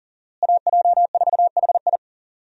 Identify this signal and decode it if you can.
Morse code, heard as a keyed tone, about 40 wpm, 700 Hz. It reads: AJ4HI
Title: USA Call Signs - 5 Characters - Repeated 3x - 40wpm